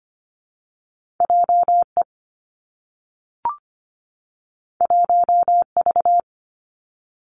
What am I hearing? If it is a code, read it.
Morse code, heard as a keyed tone, about 25 wpm, 700 Hz. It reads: J E 1 V
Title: Two Challenging Letter-Number Combinations 25wpm